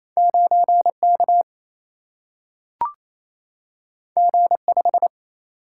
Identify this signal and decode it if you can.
Morse code, heard as a keyed tone, about 28 wpm, 700 Hz. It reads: 9K G5
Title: Single-Letter Single-Number 28wpm